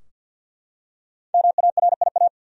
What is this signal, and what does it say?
Morse code, heard as a keyed tone, about 52 wpm, 700 Hz. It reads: MARIA
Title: USA Common Names - Max 5 Letters - 50wpm